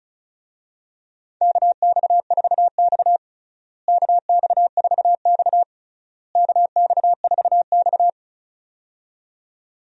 Morse code, heard as a keyed tone, about 36 wpm, 700 Hz. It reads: KX4X KX4X KX4X
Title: USA Call Signs - 4 Characters - Repeated 3x - 35wpm